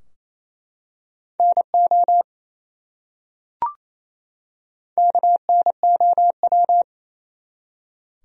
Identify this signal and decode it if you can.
Morse code, heard as a keyed tone, about 28 wpm, 700 Hz. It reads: NO KNOW
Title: Top 100 Words 28wpm